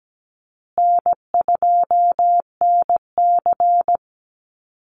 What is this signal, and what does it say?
Morse code, heard as a keyed tone, about 17 wpm, 700 Hz. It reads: N2NC
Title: Call Signs Repeated 3x - 17wpm